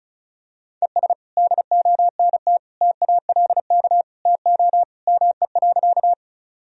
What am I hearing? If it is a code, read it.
Morse code, heard as a keyed tone, about 36 wpm, 700 Hz. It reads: ES DONT TALK TO ME.
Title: Sentences from Top 400 Words and Abbreviations 35wpm